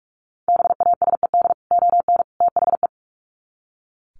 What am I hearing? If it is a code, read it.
Morse code, heard as a keyed tone, about 44 wpm, 700 Hz. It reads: BASED ON THE
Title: Sets of 3 Words 45wpm